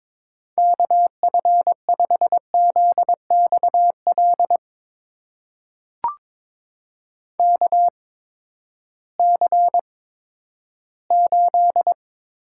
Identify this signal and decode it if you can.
Morse code, heard as a keyed tone, about 22 wpm, spaced as for 12 wpm, 700 Hz. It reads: KF5ZXL KC8
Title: USA Call Signs - 6 Characters - 8x Character Spacing - 22wpm